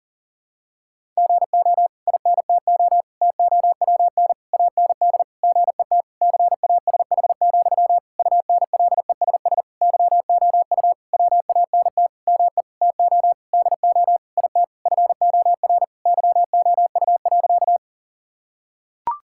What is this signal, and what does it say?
Morse code, heard as a keyed tone, about 40 wpm, 700 Hz. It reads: GO INTO TOWN AND GET CASH, UNLESS YOU WANT ME TO DO IT FOR YOU.
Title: Sentences from Top 1000 Words - Review Part 6 - 40wpm